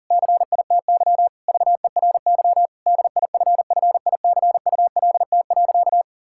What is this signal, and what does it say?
Morse code, heard as a keyed tone, about 37 wpm, 700 Hz. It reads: CITY VERY DIFFICULT.
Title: Sentences from Top 1000 Words - Review Part 6 - 40wpm